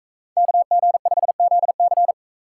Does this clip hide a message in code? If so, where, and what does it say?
Yes: Morse code from 0.4 to 2.1 s, heard as a keyed tone, about 42 wpm, 700 Hz. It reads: KG5ZC